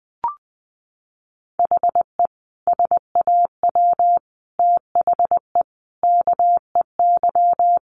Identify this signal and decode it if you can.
Morse code, heard as a keyed tone, about 20 wpm, 700 Hz. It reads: HE SAW THE KEY
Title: Sentences from Top 1300 Words 20wpm